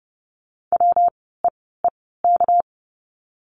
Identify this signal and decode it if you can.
Morse code, heard as a keyed tone, about 30 wpm, spaced as for 17 wpm, 700 Hz. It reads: WEEK
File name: course / 30:10wpm / 237 Lesson 38f - Letter q - Words 30wpm